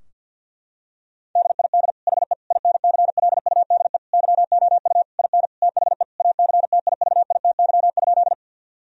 Morse code, heard as a keyed tone, about 50 wpm, 700 Hz. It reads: DID HE INCLUDE YOU IN THE ACTIVITY?
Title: Sentences from Top 1000 Words - Review Part 4 - 50wpm